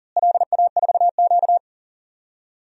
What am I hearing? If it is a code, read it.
Morse code, heard as a keyed tone, about 40 wpm, 700 Hz. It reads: LA4Q
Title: DXCC-100 Call Signs - 4 Characters - Repeated 2x - 40wpm